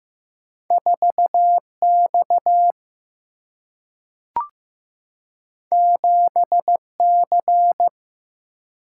Morse code, heard as a keyed tone, about 15 wpm, 700 Hz. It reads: 4X 7C
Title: Single-Letter Single-Number 15wpm